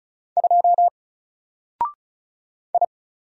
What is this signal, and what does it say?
Morse code, heard as a keyed tone, about 33 wpm, 700 Hz. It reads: 2 I
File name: Single-Letter-Number-35wpm